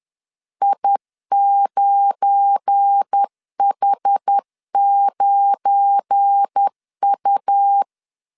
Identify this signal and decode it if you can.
Morse code, heard as a keyed tone, about 10 wpm, 800 Hz. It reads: I9H9U
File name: LEZIONI / 7 GF9Z / 7 GF9 10wpm 2747